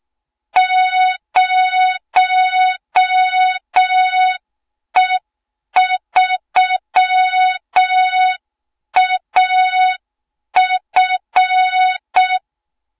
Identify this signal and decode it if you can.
Morse code, heard as a keyed tone, about 6 wpm, 3700 Hz. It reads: 0E3AF
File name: Morze-HA2MN / hang / 20 GYAKORLAT